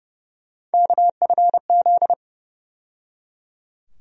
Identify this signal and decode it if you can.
Morse code, heard as a keyed tone, about 31 wpm, 700 Hz. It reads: KFZ
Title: USA Call Sign Postfixes - 3 Letters 30wpm